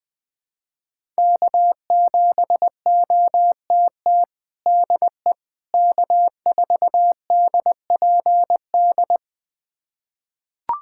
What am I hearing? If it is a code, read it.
Morse code, heard as a keyed tone, about 21 wpm, 700 Hz. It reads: K7OTT DE K4DPD
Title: USA Callsign DE Callsign - 5 Characters - 20wpm